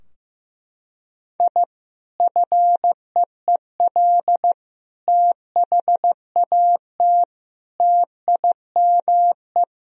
Morse code, heard as a keyed tone, about 15 wpm, 700 Hz. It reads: I FEEL THAT TIME
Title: Sentences from Top 1000 Words - Review Part 3 - 15wpm